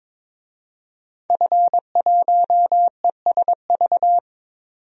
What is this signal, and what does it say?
Morse code, heard as a keyed tone, about 22 wpm, 700 Hz. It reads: F1ESV
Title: DXCC-100 Call Signs - 5 Characters - 22wpm